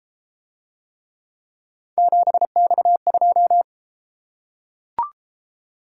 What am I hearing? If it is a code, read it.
Morse code, heard as a keyed tone, about 30 wpm, 700 Hz. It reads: 7X2